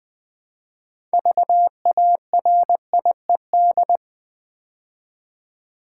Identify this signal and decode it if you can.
Morse code, heard as a keyed tone, about 20 wpm, 700 Hz. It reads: VARIED
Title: Build A Word - 3 Letters At A Time - 6-Letter Words - 20wpm